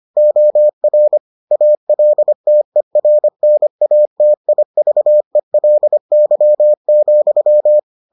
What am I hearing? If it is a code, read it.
Morse code, heard as a keyed tone, about 25 wpm, 600 Hz. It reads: OR ALTERNATIVELY,